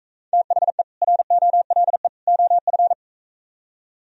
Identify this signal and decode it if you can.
Morse code, heard as a keyed tone, about 44 wpm, 700 Hz. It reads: THE ROLE OF